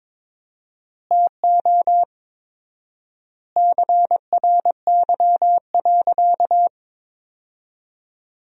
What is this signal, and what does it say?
Morse code, heard as a keyed tone, about 22 wpm, 700 Hz. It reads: TO CRY.
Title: 4x Word-Spacing - Sentences from Top 500 Words - 22wpm